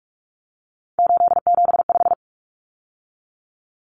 Morse code, heard as a keyed tone, about 44 wpm, 700 Hz. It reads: 875
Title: Numbers - 3-Digit 45wpm